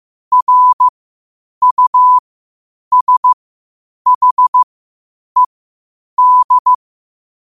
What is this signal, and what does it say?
Morse code, heard as a keyed tone, about 15 wpm, spaced as for 8 wpm, 1000 Hz. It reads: RUSHED
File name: qotdfiles / 8wpm